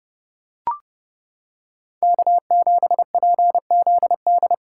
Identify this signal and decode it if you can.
Morse code, heard as a keyed tone, about 29 wpm, 700 Hz. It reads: K7PZD